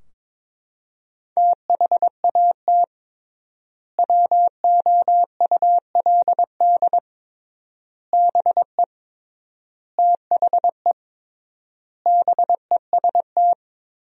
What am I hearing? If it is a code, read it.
Morse code, heard as a keyed tone, about 22 wpm, 700 Hz. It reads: THAT WOULD BE THE BEST
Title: 3x Word-Spacing - Sentences from Top 400 Words - 22wpm